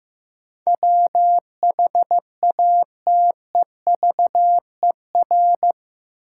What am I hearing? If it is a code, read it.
Morse code, heard as a keyed tone, about 15 wpm, 700 Hz. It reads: WHATEVER
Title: Sentences from Top 1100 Words - New Word Drill - 15wpm